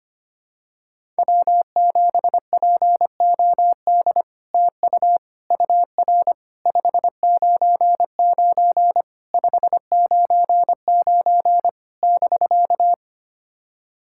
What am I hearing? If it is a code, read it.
Morse code, heard as a keyed tone, about 25 wpm, 700 Hz. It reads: W7POD TU UR 599 599 <BK>
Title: SOTA QSO Practice 25wpm